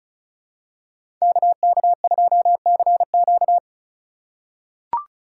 Morse code, heard as a keyed tone, about 36 wpm, 700 Hz. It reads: KK2CQ
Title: USA Call Signs - 5 Characters - 2x Character Spacing - 35wpm